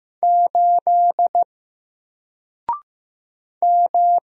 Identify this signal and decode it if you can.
Morse code, heard as a keyed tone, about 15 wpm, 700 Hz. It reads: 8 M
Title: Single Character (Letter:Number:Punctuation:Prosign) 15wpm